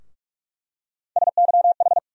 Morse code, heard as a keyed tone, about 46 wpm, 700 Hz. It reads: SYH
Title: USA Call Sign Postfixes - 3 Letters 45wpm